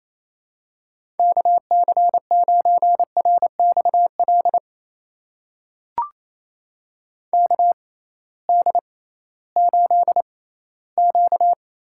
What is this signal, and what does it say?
Morse code, heard as a keyed tone, about 27 wpm, spaced as for 16 wpm, 700 Hz. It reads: KC9RXL KD8Q
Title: USA Call Signs - 6 Characters - 6x Character Spacing - 28wpm